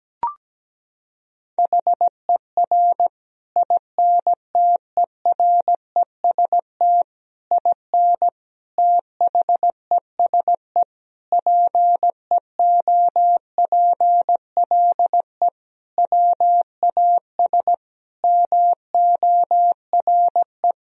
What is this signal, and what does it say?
Morse code, heard as a keyed tone, about 17 wpm, 700 Hz. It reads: HER INTEREST IN THESE PEOPLE WAS MORE
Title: Sentences from Top 1000 Words - Review Part 2 - 17wpm